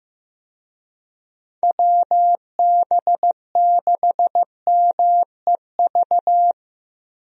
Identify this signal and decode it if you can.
Morse code, heard as a keyed tone, about 15 wpm, 700 Hz. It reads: WB6MEV
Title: USA Call Signs - 6 Characters - Repeated 3x - 15wpm